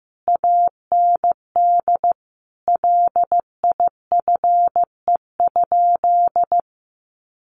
Morse code, heard as a keyed tone, about 15 wpm, 700 Hz. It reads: AND LIFE?
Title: Sentences from Top 1300 Words 15wpm